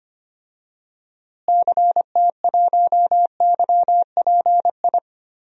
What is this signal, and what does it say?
Morse code, heard as a keyed tone, about 25 wpm, 700 Hz. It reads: CT1YPI
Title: DXCC-100 Call Signs - 6 Characters - Repeated 3x - 25wpm